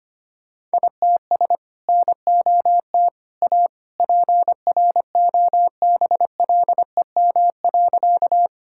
Morse code, heard as a keyed tone, about 25 wpm, 700 Hz. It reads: ITS NOT A PROBLEM.
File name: practice / 25wpm / Sentences with 2-Word Phrases - 25wpm